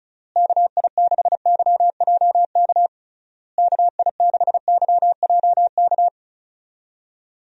Morse code, heard as a keyed tone, about 36 wpm, 700 Hz. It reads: KI6YJK KI6YJK
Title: USA Call Signs - 6 Characters - Repeated 3x - 35wpm